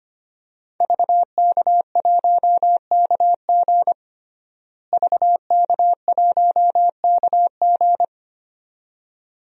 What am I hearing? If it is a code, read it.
Morse code, heard as a keyed tone, about 25 wpm, 700 Hz. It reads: VK1KG VK1KG
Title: DXCC-100 Call Signs - 5 Characters - Repeated 2x - 25wpm